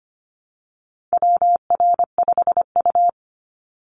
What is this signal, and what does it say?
Morse code, heard as a keyed tone, about 25 wpm, 700 Hz. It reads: WR5U
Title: Call Signs - 25wpm